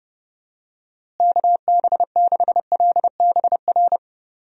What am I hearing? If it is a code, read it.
Morse code, heard as a keyed tone, about 30 wpm, 700 Hz. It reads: KB6LBR
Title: USA Call Signs - 6 Characters - Repeated 2x - 30wpm